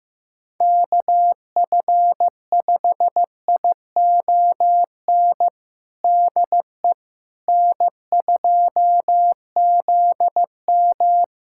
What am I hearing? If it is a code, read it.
Morse code, heard as a keyed tone, about 16 wpm, 700 Hz. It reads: KF5ION DE N2ZM
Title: USA Callsign DE Callsign 15wpm